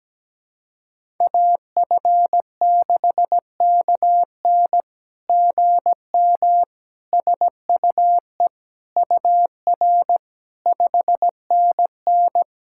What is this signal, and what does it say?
Morse code, heard as a keyed tone, about 17 wpm, 700 Hz. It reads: AF6KN GM SUE UR 5NN
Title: SOTA QSO Practice 17wpm